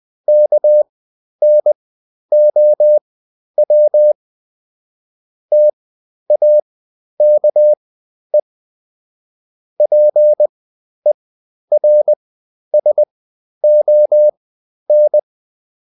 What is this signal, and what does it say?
Morse code, heard as a keyed tone, about 20 wpm, spaced as for 11 wpm, 600 Hz. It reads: KNOW TAKE PERSON
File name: Top-100-Words-10